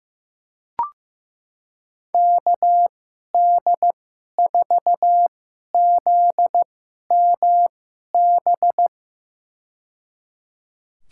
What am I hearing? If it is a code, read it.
Morse code, heard as a keyed tone, about 15 wpm, spaced as for 11 wpm, 700 Hz. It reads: KD4ZMB
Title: USA Call Signs - 6 Characters - 2x Character Spacing - 15wpm